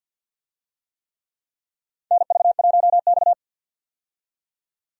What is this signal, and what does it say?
Morse code, heard as a keyed tone, about 50 wpm, 700 Hz. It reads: NV1X
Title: USA Call Signs - 4 Characters - 50wpm